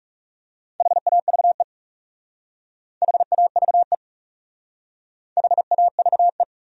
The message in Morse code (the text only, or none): HAVE HAVE HAVE